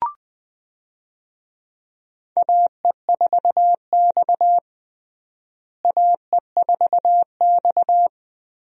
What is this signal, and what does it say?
Morse code, heard as a keyed tone, about 20 wpm, 700 Hz. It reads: AE4X AE4X
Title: USA Call Signs - 4 Characters - Repeated 3x - 20wpm